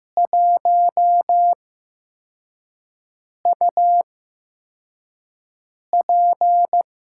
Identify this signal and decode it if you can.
Morse code, heard as a keyed tone, about 15 wpm, spaced as for 4 wpm, 700 Hz. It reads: 1UP
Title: USA Call Signs - 8x Character Spacing - 15wpm